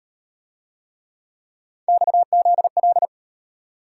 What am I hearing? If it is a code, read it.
Morse code, heard as a keyed tone, about 35 wpm, 700 Hz. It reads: XZL